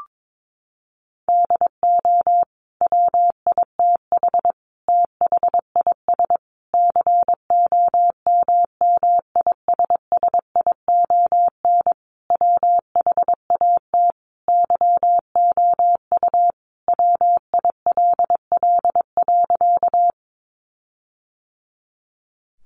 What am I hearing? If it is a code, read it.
Morse code, heard as a keyed tone, about 22 wpm, 700 Hz. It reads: DO WITH THIS COMMISSION WHAT YOU WILL.
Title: Sentences from Top 1400 Words 22wpm